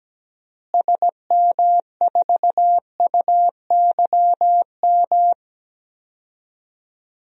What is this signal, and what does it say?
Morse code, heard as a keyed tone, about 17 wpm, 700 Hz. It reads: SM4UYM
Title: DXCC-100 Call Signs - 6 Characters - Repeated 2x - 17wpm